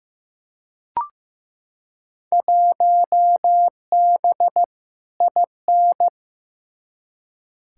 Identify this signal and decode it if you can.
Morse code, heard as a keyed tone, about 15 wpm, 700 Hz. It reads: 1B IN
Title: Field Day Exchanges - 15wpm